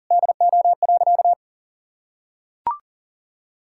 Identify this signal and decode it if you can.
Morse code, heard as a keyed tone, about 40 wpm, 700 Hz. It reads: DO.
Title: Sentences from Top 1000 Words - Review Part 6 - 40wpm